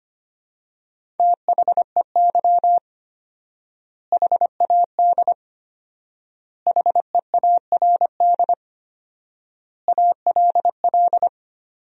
Morse code, heard as a keyed tone, about 25 wpm, 700 Hz. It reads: THEY HAD HEARD ALL
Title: 4x Word-Spacing - Sentences from Top 400 Words - 25wpm